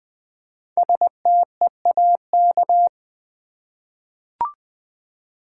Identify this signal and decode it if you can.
Morse code, heard as a keyed tone, about 20 wpm, 700 Hz. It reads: STEAK
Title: Five Letter Words - 20wpm